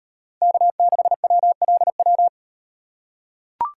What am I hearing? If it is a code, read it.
Morse code, heard as a keyed tone, about 38 wpm, 700 Hz. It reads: K6WLW